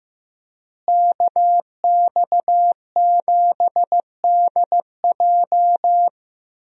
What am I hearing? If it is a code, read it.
Morse code, heard as a keyed tone, about 15 wpm, 700 Hz. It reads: KX7DJ
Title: USA Call Signs - 5 Characters - Repeated 3x - 15wpm